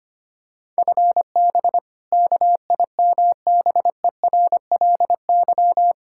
Morse code, heard as a keyed tone, about 25 wpm, 700 Hz. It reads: FB KIMBERLY
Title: Practice QSO - Exchange 3 - More Info - 25wpm